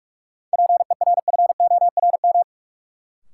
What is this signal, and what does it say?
Morse code, heard as a keyed tone, about 44 wpm, 700 Hz. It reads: PERFORM
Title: Sentences from Top 1000 Words - Review Part 5 - New Word Drill - 45wpm